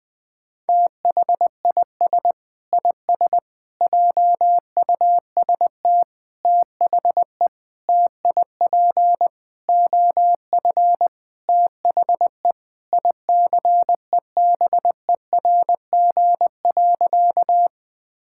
Morse code, heard as a keyed tone, about 21 wpm, 700 Hz. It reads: THIS IS JUST THE TIP OF THE ICEBERG.
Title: Sentences with 6-Word Phrases - Spoken Beginning - 20wpm